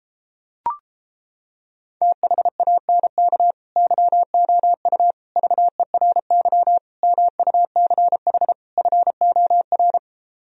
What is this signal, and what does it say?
Morse code, heard as a keyed tone, about 31 wpm, 700 Hz. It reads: THANK YOU VERY MUCH FOR